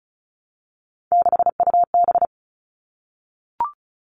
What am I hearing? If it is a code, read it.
Morse code, heard as a keyed tone, about 34 wpm, 700 Hz. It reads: 6UB
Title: All Cut Numbers - 3-Digit 35wpm